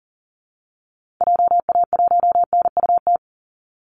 Morse code, heard as a keyed tone, about 40 wpm, 700 Hz. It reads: JA1NUT
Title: Call Signs Repeated 2x - 40wpm